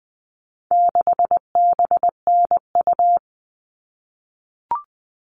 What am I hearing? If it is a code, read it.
Morse code, heard as a keyed tone, about 20 wpm, 700 Hz. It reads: 6BNU
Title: All Cut Numbers - 4-Digit 20wpm